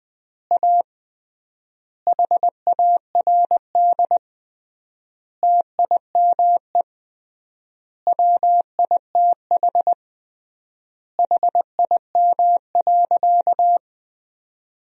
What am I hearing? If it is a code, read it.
Morse code, heard as a keyed tone, about 20 wpm, 700 Hz. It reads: A HARD TIME WITH HIM.
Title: 3x Word-Spacing - Sentences from Top 300 Words - 20wpm